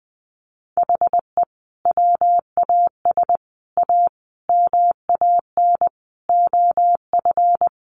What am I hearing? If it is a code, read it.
Morse code, heard as a keyed tone, about 20 wpm, 700 Hz. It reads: HE WAS A MAN OF
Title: Sentences from Top 1200 Words 20wpm